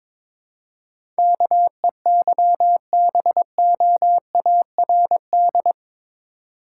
Morse code, heard as a keyed tone, about 21 wpm, 700 Hz. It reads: KEYBOARD
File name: practice / 22wpm / Eight Letter Words - 22wpm